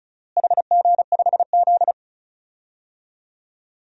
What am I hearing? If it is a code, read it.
Morse code, heard as a keyed tone, about 36 wpm, 700 Hz. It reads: HG5Z
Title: DXCC-100 Call Signs - 4 Characters - Repeated 2x - 35wpm